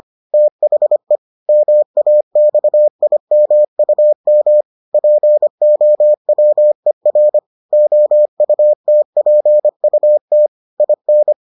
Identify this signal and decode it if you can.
Morse code, heard as a keyed tone, about 25 wpm, 600 Hz. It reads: THE MAXIMUM POWER OUTPUT IN